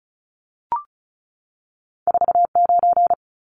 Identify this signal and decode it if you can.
Morse code, heard as a keyed tone, about 36 wpm, 700 Hz. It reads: 49
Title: Numbers - 2-Digit 35wpm